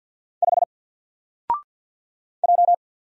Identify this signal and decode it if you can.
Morse code, heard as a keyed tone, about 50 wpm, 700 Hz. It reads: 5 J
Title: Single Letter-Number 50wpm